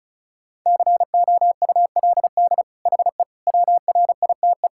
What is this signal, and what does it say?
Morse code, heard as a keyed tone, about 36 wpm, 700 Hz. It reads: COULD HE WRITE
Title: Sentences from Top 1000 Words - Review Part 6 - 35wpm